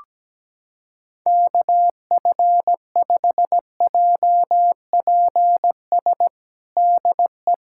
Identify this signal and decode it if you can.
Morse code, heard as a keyed tone, about 17 wpm, 700 Hz. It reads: KF5JPS DE
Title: USA Callsign DE Callsign 17wpm